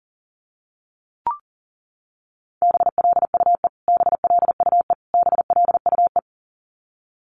Morse code, heard as a keyed tone, about 40 wpm, 700 Hz. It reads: BLUE BLUE BLUE
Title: Top 500 Words Repeated 40wpm